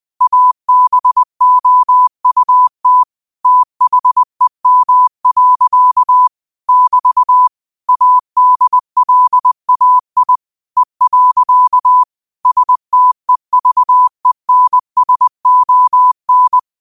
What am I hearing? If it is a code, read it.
Morse code, heard as a keyed tone, about 20 wpm, 1000 Hz. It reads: ABOUT THEM. <BT> ADLAI E. STEVENSON